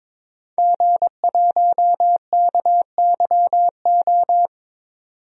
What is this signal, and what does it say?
Morse code, heard as a keyed tone, about 22 wpm, 700 Hz. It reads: G1KYO
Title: DXCC-100 Call Signs - 5 Characters - 22wpm